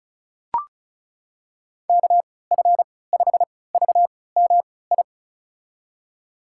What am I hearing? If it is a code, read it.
Morse code, heard as a keyed tone, about 36 wpm, spaced as for 20 wpm, 700 Hz. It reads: KF5VMI
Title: USA Call Signs - 6 Characters - 3x Character Spacing - 35wpm